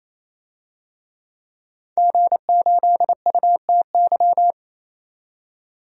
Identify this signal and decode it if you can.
Morse code, heard as a keyed tone, about 28 wpm, 700 Hz. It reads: G8UTY
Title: DXCC-100 Call Signs - 5 Characters - 28wpm